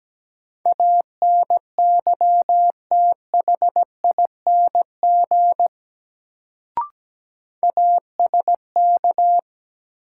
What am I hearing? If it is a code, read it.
Morse code, heard as a keyed tone, about 17 wpm, 700 Hz. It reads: ANYTHING ASK
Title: Sets of 4 Words - New Word Drill - 17wpm